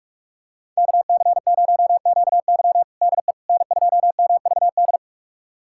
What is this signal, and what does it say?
Morse code, heard as a keyed tone, about 46 wpm, 700 Hz. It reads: KC0QY DE N2MVD
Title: USA Callsign DE Callsign - 5 Characters - 45wpm